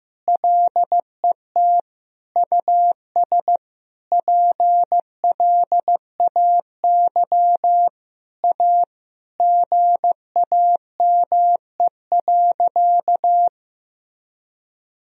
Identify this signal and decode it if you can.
Morse code, heard as a keyed tone, about 15 wpm, 700 Hz. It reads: LET US PLAY A GAME.
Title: Sentences from Top 1000 Words - Review Part 4 - 15wpm